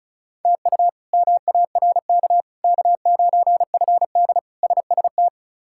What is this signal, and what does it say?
Morse code, heard as a keyed tone, about 36 wpm, 700 Hz. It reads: TU MARK K9FD SST
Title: K1USN SST QSO Practice - 35wpm